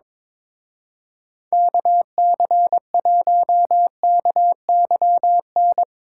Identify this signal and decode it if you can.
Morse code, heard as a keyed tone, about 22 wpm, 700 Hz. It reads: KC1KYN
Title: Speed Racing - USA 2x3 Call Signs 22wpm